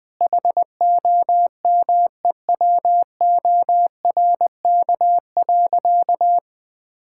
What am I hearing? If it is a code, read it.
Morse code, heard as a keyed tone, about 20 wpm, 700 Hz. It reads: HOMEWORK.
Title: Sentences with 4-Word Phrases - Spoken Beginning - 20wpm